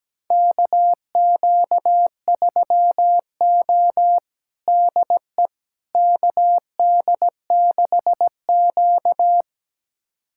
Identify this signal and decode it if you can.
Morse code, heard as a keyed tone, about 17 wpm, 700 Hz. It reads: KQ3O DE KD6Q
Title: USA Callsign DE Callsign - 4 Characters - 17wpm